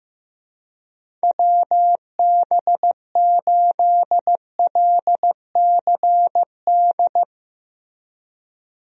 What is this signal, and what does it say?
Morse code, heard as a keyed tone, about 16 wpm, 700 Hz. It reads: WB8LCD
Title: USA Call Signs - 6 Characters - Repeated 2x - 15wpm